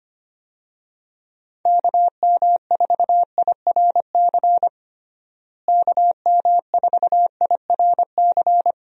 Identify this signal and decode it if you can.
Morse code, heard as a keyed tone, about 25 wpm, 700 Hz. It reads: KM4IRC KM4IRC
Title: USA Call Signs - 6 Characters - Repeated 3x - 25wpm